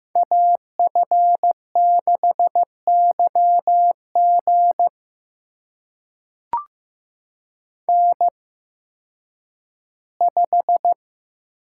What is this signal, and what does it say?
Morse code, heard as a keyed tone, about 15 wpm, spaced as for 10 wpm, 700 Hz. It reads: AF6YG N5
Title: USA Call Signs - 5 Characters - 8x Character Spacing - 15wpm